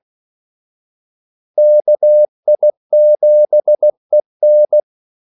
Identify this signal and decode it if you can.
Morse code, heard as a keyed tone, about 16 wpm, 600 Hz. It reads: KI7EN
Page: Kaldesignaler 16 wpm